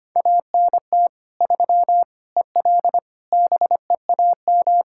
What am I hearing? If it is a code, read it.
Morse code, heard as a keyed tone, about 25 wpm, 700 Hz. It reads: ANT 3 EL BEAM
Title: K1USN SST QSO Practice 25wpm